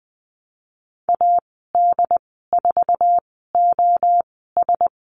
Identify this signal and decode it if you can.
Morse code, heard as a keyed tone, about 20 wpm, spaced as for 15 wpm, 700 Hz. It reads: AD4OS
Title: 082 Lesson 14g - Number 2 - Call Signs 20wpm